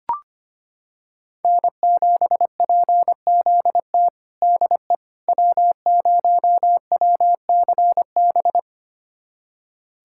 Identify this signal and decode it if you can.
Morse code, heard as a keyed tone, about 25 wpm, 700 Hz. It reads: N7PZT DE W0WCB
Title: USA Callsign DE Callsign 25wpm